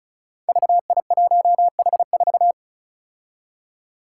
Morse code, heard as a keyed tone, about 36 wpm, 700 Hz. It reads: VI1H4
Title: Five Challenging Letter-Number Combinations 35wpm